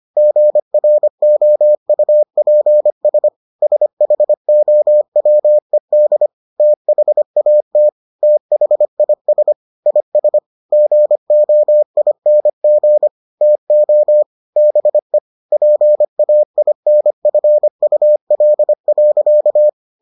Morse code, heard as a keyed tone, about 25 wpm, 600 Hz. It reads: GROUPS SHOWED THAT THIS IS GOING TO BE PAINFUL.